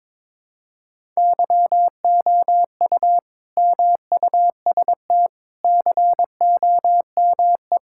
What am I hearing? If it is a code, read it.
Morse code, heard as a keyed tone, about 22 wpm, 700 Hz. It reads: YOU MUST COME
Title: Sentences from Top 1000 Words - Review Part 5 - 22wpm